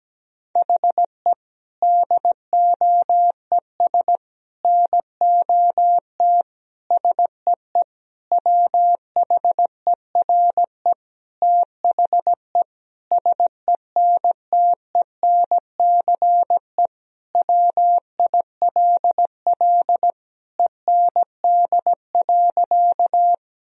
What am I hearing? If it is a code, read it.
Morse code, heard as a keyed tone, about 17 wpm, 700 Hz. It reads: HE DOES NOT SEE WHERE THE SENTENCE WILL END.
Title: Sentences from Top 1000 Words - Review Part 1 - 17wpm